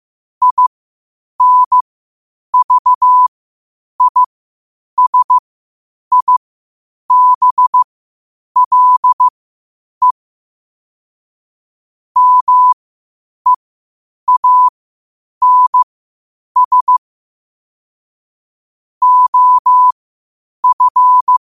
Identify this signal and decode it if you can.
Morse code, heard as a keyed tone, about 15 wpm, spaced as for 8 wpm, 1000 Hz. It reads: INVISIBLE MEANS OF
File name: qotdfiles / 8wpm